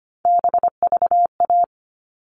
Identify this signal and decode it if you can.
Morse code, heard as a keyed tone, about 25 wpm, 700 Hz. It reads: BVA
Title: All Cut Numbers - 3-Digit 25wpm